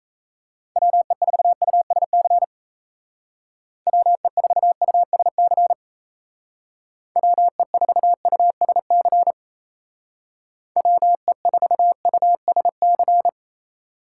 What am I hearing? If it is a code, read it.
Morse code, heard as a keyed tone, about 33 wpm, 700 Hz. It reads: WE4USC WE4USC WE4USC WE4USC
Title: Speed Racing - USA 2x3 Call Signs 28wpm